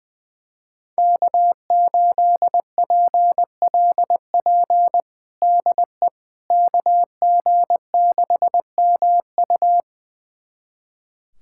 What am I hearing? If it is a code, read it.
Morse code, heard as a keyed tone, about 20 wpm, 700 Hz. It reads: K8PLP DE KG6MU
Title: USA Callsign DE Callsign - 5 Characters - 20wpm